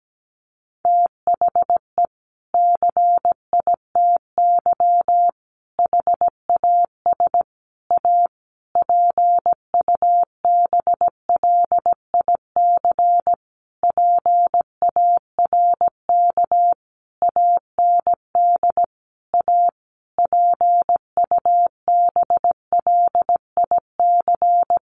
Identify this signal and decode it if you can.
Morse code, heard as a keyed tone, about 17 wpm, 700 Hz. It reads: THE CITY HAS A PUBLIC PARK AND A PUBLIC
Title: Sentences from Top 1300 Words 17wpm